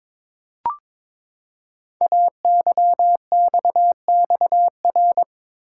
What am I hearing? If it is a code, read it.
Morse code, heard as a keyed tone, about 22 wpm, 700 Hz. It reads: AYXXR
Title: Code Groups - 5 Letters - 22wpm